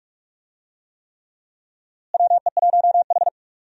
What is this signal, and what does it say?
Morse code, heard as a keyed tone, about 46 wpm, 700 Hz. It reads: WE1H
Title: USA Call Signs - 4 Characters - Repeated 3x - 45wpm